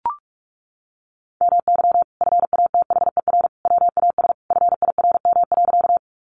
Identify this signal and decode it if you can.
Morse code, heard as a keyed tone, about 44 wpm, 700 Hz. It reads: MY FATHER WAS FIRM.
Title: Sentences from Top 1400 Words 45wpm